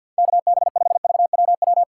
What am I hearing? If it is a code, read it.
Morse code, heard as a keyed tone, about 48 wpm, 700 Hz. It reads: KB5VWW
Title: USA Call Signs - 6 Characters - 3x Character Spacing - 50wpm